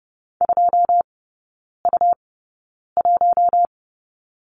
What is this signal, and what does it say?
Morse code, heard as a keyed tone, about 30 wpm, spaced as for 9 wpm, 700 Hz. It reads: 2U1
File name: Lesson 17e - Letter u - Three Characters 30wpm